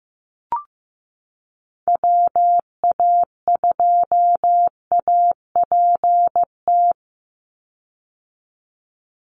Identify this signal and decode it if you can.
Morse code, heard as a keyed tone, about 15 wpm, 700 Hz. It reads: WA2APT
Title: USA Call Signs 15wpm